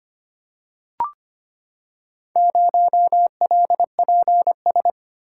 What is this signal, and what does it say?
Morse code, heard as a keyed tone, about 25 wpm, 700 Hz. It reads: 0LPS
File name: practice / 25wpm / Four Letter-Number Combinations - 25wpm